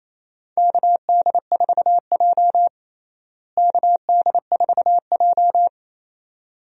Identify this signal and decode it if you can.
Morse code, heard as a keyed tone, about 28 wpm, 700 Hz. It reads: KD4J KD4J
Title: USA Call Signs - 4 Characters - Repeated 2x - 28wpm